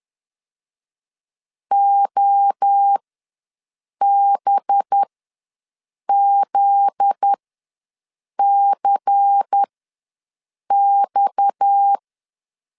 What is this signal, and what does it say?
Morse code, heard as a keyed tone, about 10 wpm, spaced as for 6 wpm, 800 Hz. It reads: OBZCX
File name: LEZIONI / 7 GF9Z / 7 GF9Z ws 10wpm 3919